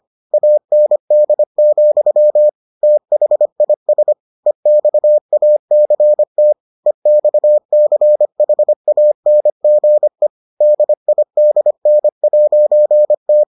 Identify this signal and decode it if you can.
Morse code, heard as a keyed tone, about 25 wpm, 600 Hz. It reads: AND, THIS EXACT EXCHANGE DIDN'T